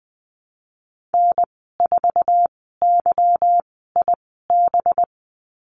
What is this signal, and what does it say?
Morse code, heard as a keyed tone, about 20 wpm, spaced as for 15 wpm, 700 Hz. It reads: N4YIB
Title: 186 Lesson 30g - Letter b - Call Signs 20wpm